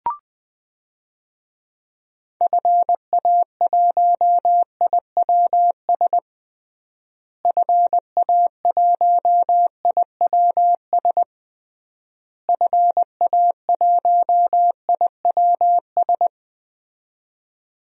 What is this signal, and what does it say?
Morse code, heard as a keyed tone, about 21 wpm, 700 Hz. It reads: FA1IWS FA1IWS FA1IWS
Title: DXCC-100 Call Signs - 6 Characters - Repeated 3x - 20wpm